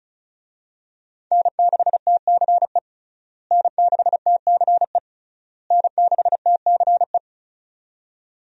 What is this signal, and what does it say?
Morse code, heard as a keyed tone, about 36 wpm, 700 Hz. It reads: N6TCE N6TCE N6TCE
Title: USA Call Signs - 5 Characters - Repeated 3x - 35wpm